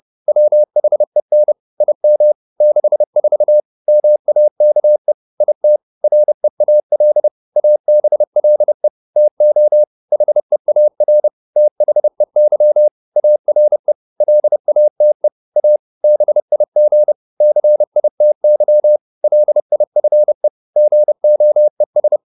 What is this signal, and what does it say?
Morse code, heard as a keyed tone, about 30 wpm, 600 Hz. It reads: WHEN IM 64 MAKE IT REAL ABLE TO HEAR THEY ARE LATE A BIG CITY LIFE GOES